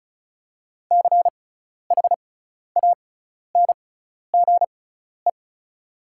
Morse code, heard as a keyed tone, about 36 wpm, spaced as for 12 wpm, 700 Hz. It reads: CHANGE
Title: 6x Character Spacing - Top 200 Words - 35wpm